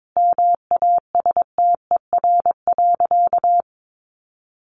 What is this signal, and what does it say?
Morse code, heard as a keyed tone, about 22 wpm, 700 Hz. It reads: MASTER.
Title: Sentences from Top 1300 Words 22wpm